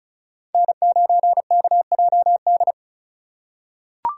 Morse code, heard as a keyed tone, about 36 wpm, 700 Hz. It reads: N9KJD
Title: USA Call Signs - 5 Characters - 2x Character Spacing - 35wpm